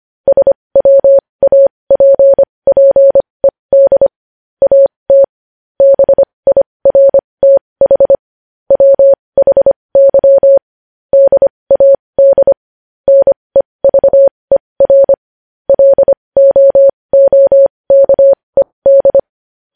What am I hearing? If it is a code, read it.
Morse code, heard as a keyed tone, about 25 wpm, spaced as for 20 wpm, 550 Hz. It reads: SWAPPED AT BIRTH WHY DAD NEVER LOOKED
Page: News headlines in Morse code at 20 wpm